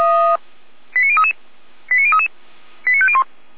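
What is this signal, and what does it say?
Five-tone selective call (ZVEI), heard as quick runs of five tones, tones of about 70 ms. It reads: 89033 89033 89501
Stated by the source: ZVEI five-tone selective call 89105, received by SDR.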